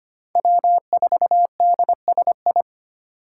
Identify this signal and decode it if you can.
Morse code, heard as a keyed tone, about 33 wpm, 700 Hz. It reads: W4DSI